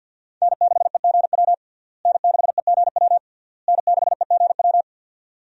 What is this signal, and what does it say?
Morse code, heard as a keyed tone, about 52 wpm, 700 Hz. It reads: N6EGW N6EGW N6EGW
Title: USA Call Signs - 5 Characters - Repeated 3x - 50wpm